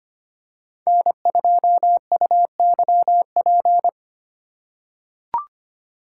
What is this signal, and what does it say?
Morse code, heard as a keyed tone, about 25 wpm, 700 Hz. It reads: N2UYP
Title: USA Call Signs - 3x Character Spacing - 25wpm